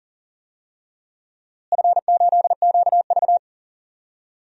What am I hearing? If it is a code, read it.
Morse code, heard as a keyed tone, about 40 wpm, 700 Hz. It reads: F8QV
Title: DXCC-100 Call Signs - 4 Characters - Repeated 3x - 40wpm